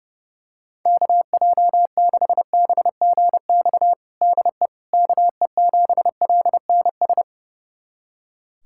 Morse code, heard as a keyed tone, about 32 wpm, 700 Hz. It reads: KJ6BGX DE KE7LNS
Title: USA Callsign DE Callsign - 6 Characters - 30wpm